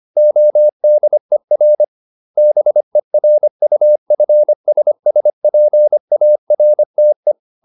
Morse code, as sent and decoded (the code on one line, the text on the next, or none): --- -.. .-. / -... .-. ..- ..-. ... ... .--. .- .-. - .
ODER BERUFSSPARTE